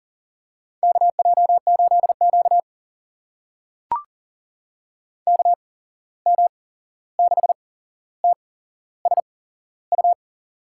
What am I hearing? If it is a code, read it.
Morse code, heard as a keyed tone, about 40 wpm, spaced as for 15 wpm, 700 Hz. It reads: KJ8Q KM6TSU
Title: USA Call Signs - 8x Character Spacing - 40wpm